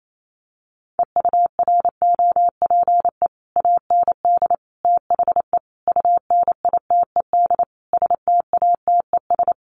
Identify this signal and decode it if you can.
Morse code, heard as a keyed tone, about 28 wpm, 700 Hz. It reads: EUROPE AND THE UNITED STATES
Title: Sets of 5 Words Encore2 28wpm